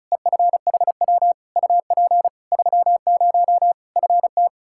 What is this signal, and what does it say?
Morse code, heard as a keyed tone, about 34 wpm, 700 Hz. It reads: EFHW UP 30 FT
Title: K1USN SST QSO Practice 35wpm